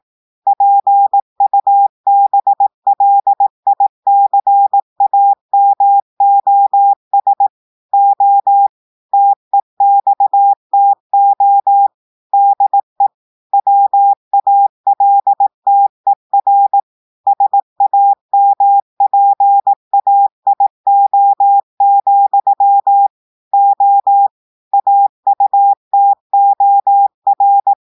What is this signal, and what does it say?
Morse code, heard as a keyed tone, about 18 wpm, 800 Hz. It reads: PUBLICAMOS O TEXTO DE WALTER SAMPAIO, O AUTOR
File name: audios / portugais / portugais texte en clair 1 vitesse 18 mots minutes